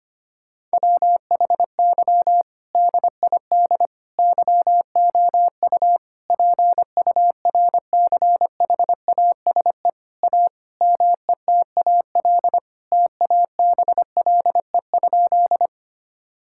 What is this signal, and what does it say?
Morse code, heard as a keyed tone, about 25 wpm, 700 Hz. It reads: WHY DID YOU PURCHASE A METAL TABLE?
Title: Sentences from Top 1000 Words - Review Part 6 - 25wpm